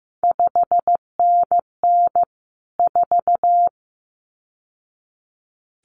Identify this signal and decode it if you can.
Morse code, heard as a keyed tone, about 15 wpm, 700 Hz. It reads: 5NN 4
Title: CQ WW DX Contest Exchange - 15wpm